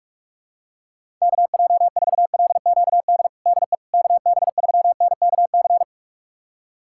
Morse code, heard as a keyed tone, about 44 wpm, 690 Hz. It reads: KJ4LQD DE KB3NKC
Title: USA Callsign DE Callsign - 6 Characters - 45wpm